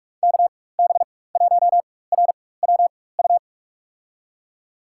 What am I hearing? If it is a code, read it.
Morse code, heard as a keyed tone, about 45 wpm, spaced as for 21 wpm, 700 Hz. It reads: KB1RWU
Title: USA Call Signs - 4x Character Spacing - 45wpm